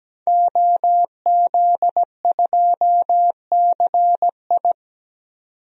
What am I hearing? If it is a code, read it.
Morse code, heard as a keyed tone, about 17 wpm, 700 Hz. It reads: OZ2CI
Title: DXCC-100 Call Signs - 5 Characters - 17wpm